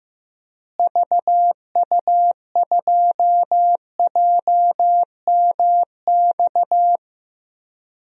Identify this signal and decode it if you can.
Morse code, heard as a keyed tone, about 15 wpm, 700 Hz. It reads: VU2JMX
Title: DXCC-100 Call Signs - 6 Characters - Repeated 3x - 15wpm